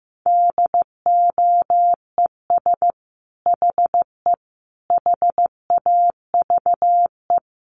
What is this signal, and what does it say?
Morse code, heard as a keyed tone, about 15 wpm, 700 Hz. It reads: DOES HE HAVE
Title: Sentences from Top 1200 Words 15wpm